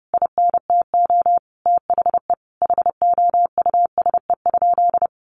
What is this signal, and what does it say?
Morse code, heard as a keyed tone, about 30 wpm, 700 Hz. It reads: INTO THE HOUSE?
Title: Sentences from Top 1300 Words 30wpm